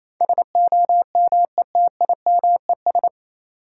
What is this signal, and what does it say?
Morse code, heard as a keyed tone, about 28 wpm, 700 Hz. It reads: SOMETIMES
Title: Sentences from Top 600 Words - New Word Drill - 28wpm